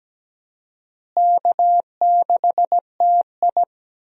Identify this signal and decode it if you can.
Morse code, heard as a keyed tone, about 17 wpm, 700 Hz. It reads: K6TI
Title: USA Call Signs - 4 Characters - Repeated 2x - 17wpm